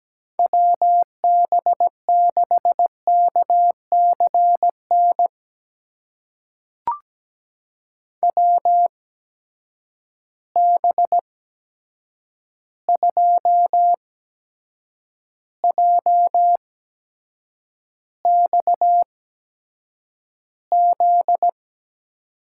Morse code, heard as a keyed tone, about 17 wpm, spaced as for 7 wpm, 700 Hz. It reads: WB6KCN WB2JXZ
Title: USA Call Signs - 6 Characters - 8x Character Spacing - 17wpm